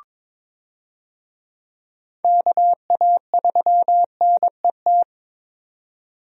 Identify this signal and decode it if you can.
Morse code, heard as a keyed tone, about 22 wpm, 700 Hz. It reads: KA3NET
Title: USA Call Signs - 6 Characters - 22wpm